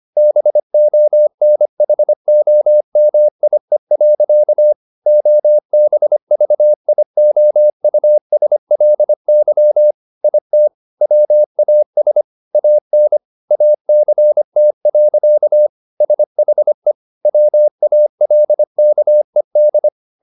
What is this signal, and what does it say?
Morse code, heard as a keyed tone, about 25 wpm, 600 Hz. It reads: BONHOMIE. OBVIOUSLY IT WAS AN ACT. SHE WALKED